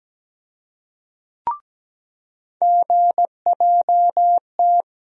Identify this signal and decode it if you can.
Morse code, heard as a keyed tone, about 17 wpm, 700 Hz. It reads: GJT
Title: USA Call Sign Postfixes - 3 Letters 17wpm